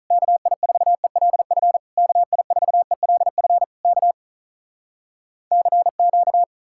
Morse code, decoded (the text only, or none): KI4ELF KI4ELF K CQ